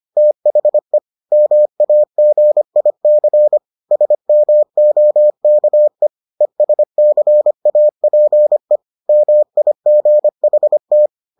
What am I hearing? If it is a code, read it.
Morse code, heard as a keyed tone, about 25 wpm, 600 Hz. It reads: THE MAGIC SMOKE ESCAPE MIGHT